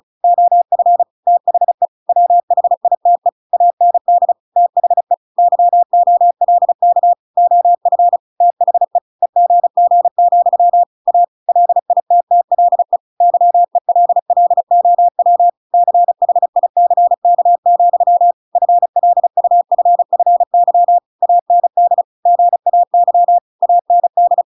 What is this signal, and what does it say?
Morse code, heard as a keyed tone, about 34 wpm, 700 Hz. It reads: OF THE WHITE AND THE YOLK OF THE EGG, A LITTLE YELLOW CHICK, FLUFFY AND GAY AND